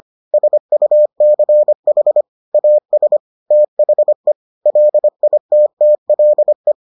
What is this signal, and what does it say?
Morse code, heard as a keyed tone, about 25 wpm, 600 Hz. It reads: SUCH AS THE LITTLE